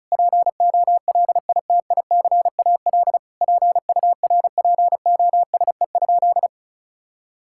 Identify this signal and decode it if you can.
Morse code, heard as a keyed tone, about 34 wpm, 700 Hz. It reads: POLITICAL PURPOSE?
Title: Sentences from Top 1000 Words - Review Part 5 - 35wpm